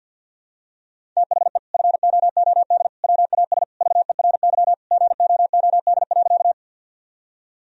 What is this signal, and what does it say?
Morse code, heard as a keyed tone, about 51 wpm, 700 Hz. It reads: THE FOOD WAS VERY GOOD.
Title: Sentences from Top 1000 Words - Review Part 1 - 50wpm